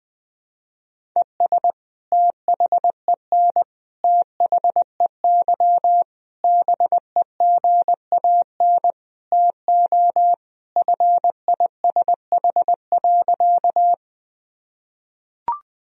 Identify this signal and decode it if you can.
Morse code, heard as a keyed tone, about 20 wpm, 700 Hz. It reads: ES THEN THEY BEGAN TO FISH.
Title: Sentences from Top 300 Words and Abbreviations 20wpm